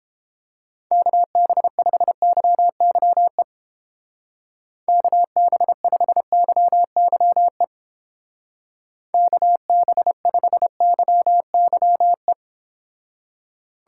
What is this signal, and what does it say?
Morse code, heard as a keyed tone, about 29 wpm, 700 Hz. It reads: KB5YYE KB5YYE KB5YYE